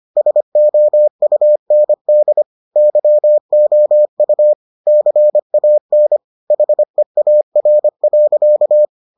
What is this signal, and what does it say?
Morse code, heard as a keyed tone, about 25 wpm, 600 Hz. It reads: SOUND YOU CAN HEAR.